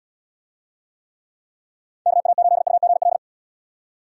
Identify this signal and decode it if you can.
Morse code, heard as a keyed tone, about 79 wpm, 700 Hz. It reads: KA8UGL